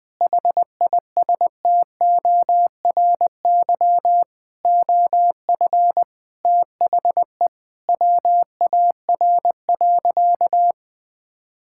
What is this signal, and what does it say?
Morse code, heard as a keyed tone, about 20 wpm, 700 Hz. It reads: HISTORY OF THE WAR.
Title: Sentences with 4-Word Phrases - 20wpm